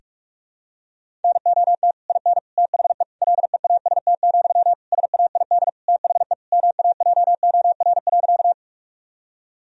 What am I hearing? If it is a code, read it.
Morse code, heard as a keyed tone, about 46 wpm, 700 Hz. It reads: NOT IN THE LEAST, SAID THE MAJOR.
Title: Sentences from Top 1000 Words - Review Part 4 - 45wpm